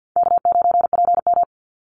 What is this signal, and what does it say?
Morse code, heard as a keyed tone, about 50 wpm, 700 Hz. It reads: K9PM
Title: NA SOTA Chaser Callsigns 50wpm